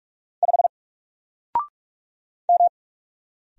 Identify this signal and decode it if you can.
Morse code, heard as a keyed tone, about 44 wpm, 700 Hz. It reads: 5 M
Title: Single Character (Letter:Number:Punctuation:Prosign) 45wpm